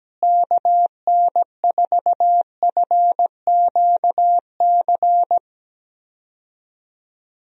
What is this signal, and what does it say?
Morse code, heard as a keyed tone, about 17 wpm, 700 Hz. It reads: KN4FQC
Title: USA Call Signs - 6 Characters - Repeated 2x - 17wpm